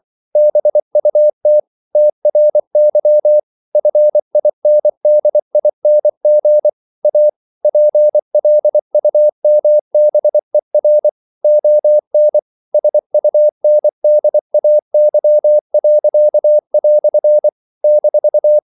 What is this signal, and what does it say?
Morse code, heard as a keyed tone, about 24 wpm, 600 Hz. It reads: BUT TRY FINDING A PLUMBER ON SUNDAY." -
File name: CW / Practice / EnglishQuotations0324